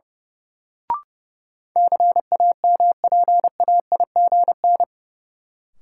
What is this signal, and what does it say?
Morse code, heard as a keyed tone, about 30 wpm, 700 Hz. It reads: CAMPAIGN
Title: Sentences from Top 1200 Words - New Word Drill - 30wpm